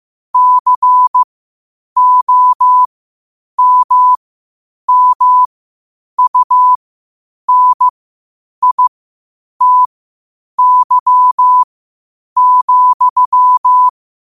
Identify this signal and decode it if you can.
Morse code, heard as a keyed tone, about 15 wpm, spaced as for 8 wpm, 1000 Hz. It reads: COMMUNITY,
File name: qotdfiles / 8wpm